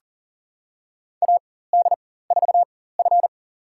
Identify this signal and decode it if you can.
Morse code, heard as a keyed tone, about 40 wpm, spaced as for 19 wpm, 700 Hz. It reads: AD4F
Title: USA Call Signs - 4x Character Spacing - 40wpm